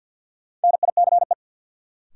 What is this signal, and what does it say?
Morse code, heard as a keyed tone, about 52 wpm, 700 Hz. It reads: NICE
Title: Sentences from Top 1100 Words - New Word Drill - 50wpm